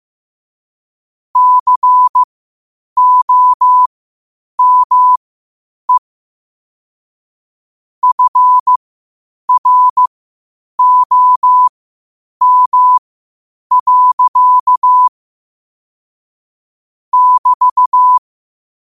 Morse code, heard as a keyed tone, about 15 wpm, spaced as for 8 wpm, 1000 Hz. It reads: COME FROM. <BT>
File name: qotdfiles / 8wpm